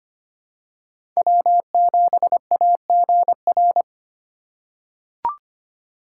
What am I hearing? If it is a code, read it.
Morse code, heard as a keyed tone, about 25 wpm, 700 Hz. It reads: W7AGR
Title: USA Call Signs - 5 Characters - 4x Character Spacing - 25wpm